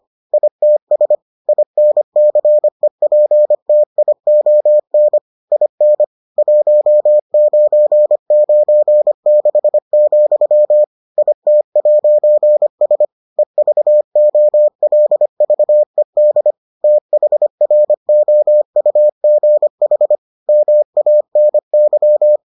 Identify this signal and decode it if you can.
Morse code, heard as a keyed tone, about 25 wpm, 600 Hz. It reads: ITS INCEPTION IN 1996, IT'S EVOLVED THROUGH MANY